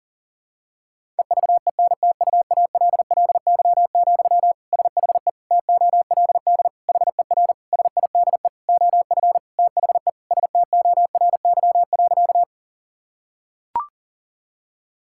Morse code, heard as a keyed tone, about 40 wpm, 700 Hz. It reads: EVENTUALLY, SHE TOLD HER SIDE OF THE STORY.
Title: Sentences from Top 1000 Words - Review Part 6 - 40wpm